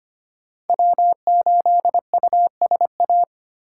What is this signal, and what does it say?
Morse code, heard as a keyed tone, about 32 wpm, 700 Hz. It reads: W8USA